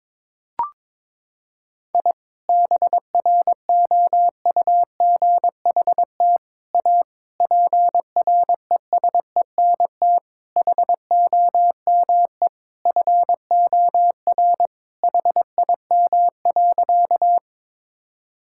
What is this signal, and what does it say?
Morse code, heard as a keyed tone, about 22 wpm, 700 Hz. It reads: I BROUGHT A PRESENT HOME FOR HIM.
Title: Sentences from Top 1000 Words - Review Part 5 - 22wpm